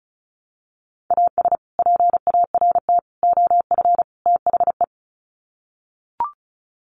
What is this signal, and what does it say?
Morse code, heard as a keyed tone, about 34 wpm, 700 Hz. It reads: AS PART OF THE
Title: Sets of 4 Words 35wpm